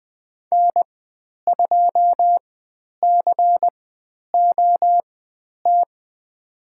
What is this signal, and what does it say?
Morse code, heard as a keyed tone, about 20 wpm, spaced as for 10 wpm, 700 Hz. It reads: N2COT
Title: 102 Lesson 17g - Letter u - Call Signs 20wpm